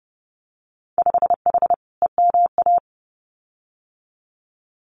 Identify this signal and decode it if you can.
Morse code, heard as a keyed tone, about 32 wpm, 700 Hz. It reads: 5H EMA
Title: Winter Field Day Exchanges Encore 30wpm